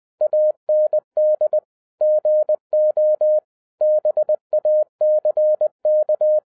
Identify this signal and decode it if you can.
Morse code, heard as a keyed tone, about 20 wpm, 610 Hz. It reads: AND GO BACK